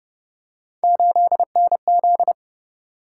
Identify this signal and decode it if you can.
Morse code, heard as a keyed tone, about 30 wpm, 700 Hz. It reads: 8NZ